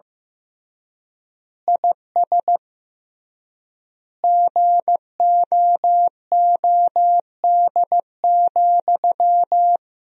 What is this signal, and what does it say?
Morse code, heard as a keyed tone, about 15 wpm, 700 Hz. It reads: IS GOOD,
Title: 3x Word-Spacing - Sentences from Top 200 Words - 15wpm